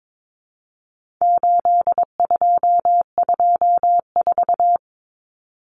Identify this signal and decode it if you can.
Morse code, heard as a keyed tone, about 22 wpm, 700 Hz. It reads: 8224
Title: Common Cut Numbers - 4-Digit 22wpm